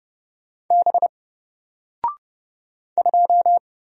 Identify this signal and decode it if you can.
Morse code, heard as a keyed tone, about 30 wpm, 700 Hz. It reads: B 2
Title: Single Character (Letter:Number:Punctuation:Prosign) 30wpm